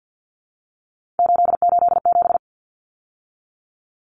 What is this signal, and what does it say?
Morse code, heard as a keyed tone, about 49 wpm, 700 Hz. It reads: 887
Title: Numbers - 3-Digit 50wpm